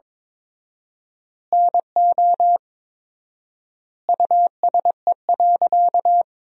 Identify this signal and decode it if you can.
Morse code, heard as a keyed tone, about 22 wpm, 700 Hz. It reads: NO USE.
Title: 4x Word-Spacing - Sentences from Top 100 Words - 22wpm